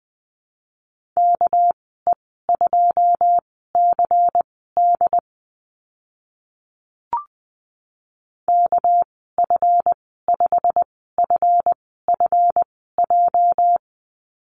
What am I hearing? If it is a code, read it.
Morse code, heard as a keyed tone, about 20 wpm, spaced as for 15 wpm, 700 Hz. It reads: KE2CD KF5FFJ
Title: 245 Lesson 39g - Letter z - Call Signs 20wpm